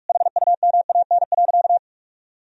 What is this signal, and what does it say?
Morse code, heard as a keyed tone, about 47 wpm, 700 Hz. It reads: HUMAN.
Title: Sentences from Top 1000 Words - Review Part 3 - 45wpm